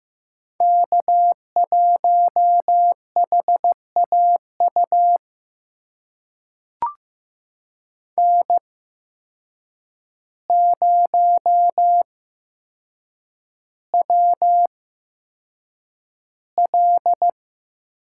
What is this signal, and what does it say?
Morse code, heard as a keyed tone, about 15 wpm, spaced as for 7 wpm, 700 Hz. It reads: K1HAU N0WL
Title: USA Call Signs - 5 Characters - 8x Character Spacing - 15wpm